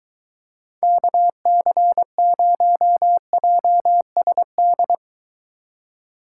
Morse code, heard as a keyed tone, about 24 wpm, 700 Hz. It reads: KC0JSD